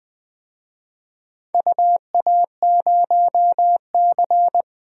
Morse code, heard as a keyed tone, about 20 wpm, 700 Hz. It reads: UA0C
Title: DXCC-100 Call Signs - 4 Characters - Repeated 2x - 20wpm